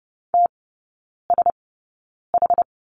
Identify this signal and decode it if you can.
Morse code, heard as a keyed tone, about 30 wpm, spaced as for 9 wpm, 700 Hz. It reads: TSH
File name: 081 Lesson 14e - Number 2 - Three Characters 30wpm